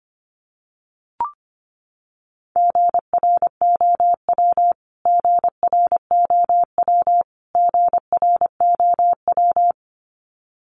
Morse code, heard as a keyed tone, about 25 wpm, 700 Hz. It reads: GROW GROW GROW
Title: Top 500 Words Repeated 25wpm